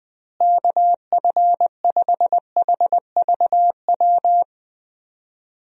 Morse code, heard as a keyed tone, about 21 wpm, 700 Hz. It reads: KF5HVW
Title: USA Call Signs - 6 Characters - Repeated 3x - 20wpm